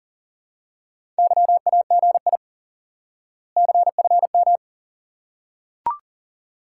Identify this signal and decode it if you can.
Morse code, heard as a keyed tone, about 40 wpm, 700 Hz. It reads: YAGI CFM
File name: IQE Course 3 - Lesson 89 - YAGI - Two Elements - 40wpm